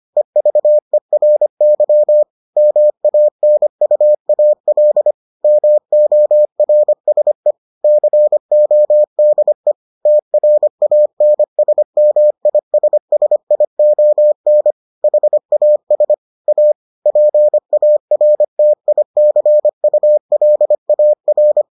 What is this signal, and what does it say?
Morse code, heard as a keyed tone, about 25 wpm, 600 Hz. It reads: EVERY MANUAL MORSE CODE TRANSMISSION HAS A PARTICULAR